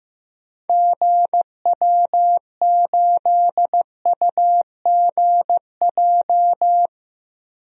Morse code, heard as a keyed tone, about 15 wpm, 700 Hz. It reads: GW8UGJ
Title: DXCC-100 Call Signs - 6 Characters - Repeated 3x - 15wpm